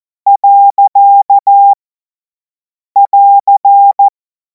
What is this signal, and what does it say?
Morse code, heard as a keyed tone, about 14 wpm, 800 Hz. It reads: . <AR>
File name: Morsekurs / Witze / Witz 14. April 2026 50 F70